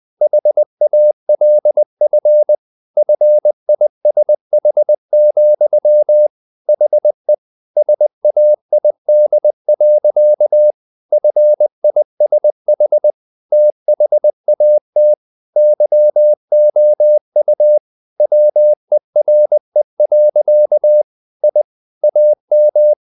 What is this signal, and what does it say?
Morse code, heard as a keyed tone, about 20 wpm, 600 Hz. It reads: HALF FISH, HE SAID. FISH THAT YOU WERE. I AM
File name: the old man and the sea part 2 20wpm